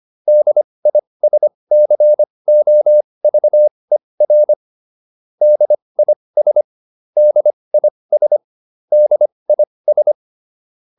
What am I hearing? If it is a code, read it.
Morse code, heard as a keyed tone, about 25 wpm, spaced as for 20 wpm, 600 Hz. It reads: DISCOVER DIS DIS DIS